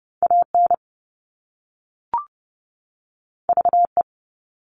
Morse code, heard as a keyed tone, about 30 wpm, 700 Hz. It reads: AN VE
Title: All Cut Numbers - 2-Digit 30wpm